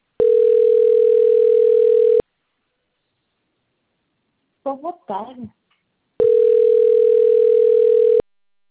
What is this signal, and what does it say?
Ringback tone (North America), 440 + 480 Hz, with speech.
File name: rg-1600-unknown- - - .54983